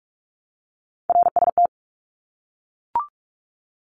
Morse code, heard as a keyed tone, about 44 wpm, 700 Hz. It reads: RST
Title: Abbreviations 45wpm